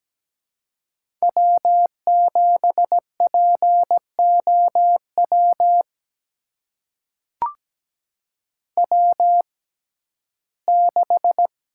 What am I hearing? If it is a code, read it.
Morse code, heard as a keyed tone, about 17 wpm, spaced as for 12 wpm, 700 Hz. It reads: W7POW W6
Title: USA Call Signs - 5 Characters - 6x Character Spacing - 17wpm